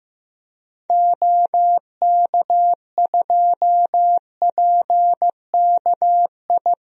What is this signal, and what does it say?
Morse code, heard as a keyed tone, about 15 wpm, 700 Hz. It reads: OK2PKI
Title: DXCC-100 Call Signs - 6 Characters - 15wpm